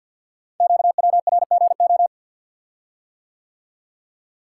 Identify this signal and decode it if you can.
Morse code, heard as a keyed tone, about 50 wpm, 700 Hz. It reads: QWRGO
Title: Code Groups - 5 Letters - 50wpm